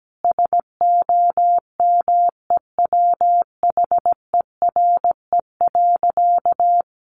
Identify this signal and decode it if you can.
Morse code, heard as a keyed tone, about 17 wpm, 700 Hz. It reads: SOMEWHERE.
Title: Sentences from Top 1400 Words 17wpm